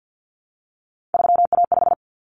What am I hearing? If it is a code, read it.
Morse code, heard as a keyed tone, about 50 wpm, 700 Hz. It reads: 3A5
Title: Common Cut Numbers - 3-Digit 50wpm